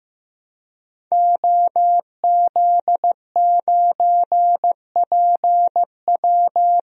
Morse code, heard as a keyed tone, about 15 wpm, 700 Hz. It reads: OZ9PW
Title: DXCC-100 Call Signs - 5 Characters - Repeated 2x - 15wpm